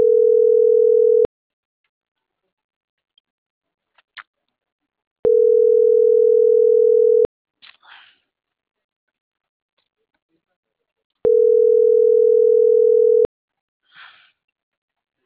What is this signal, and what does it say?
Ringback tone (North America), 440 + 480 Hz.